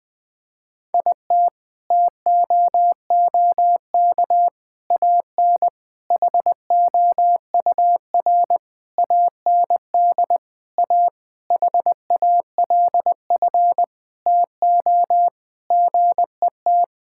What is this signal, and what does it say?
Morse code, heard as a keyed tone, about 21 wpm, 700 Hz. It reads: IT TOOK AN HOUR AND A HALF TO GET
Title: Sentences with 5-Word Phrases - Spoken Beginning - 20wpm